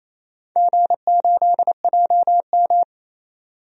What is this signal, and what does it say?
Morse code, heard as a keyed tone, about 28 wpm, 700 Hz. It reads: G8JM
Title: DXCC-100 Call Signs - 4 Characters - 28wpm